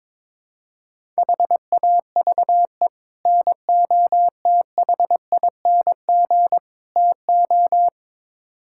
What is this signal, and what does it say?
Morse code, heard as a keyed tone, about 22 wpm, 700 Hz. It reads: HAVE NOTHING TO
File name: practice / 22wpm / Build A Phrase - 5-Word Phrases - 22wpm